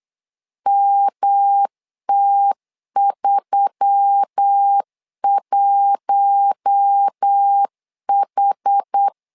Morse code, heard as a keyed tone, about 8 wpm, 800 Hz. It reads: MT31H
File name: LEZIONI / 4 Q1S3 8wpm 3037